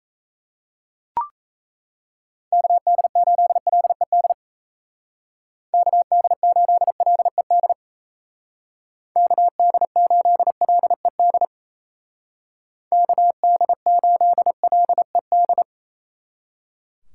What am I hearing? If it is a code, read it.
Morse code, heard as a keyed tone, about 34 wpm, 700 Hz. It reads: KD8LED KD8LED KD8LED KD8LED
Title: Speed Racing - USA 2x3 Call Signs 28wpm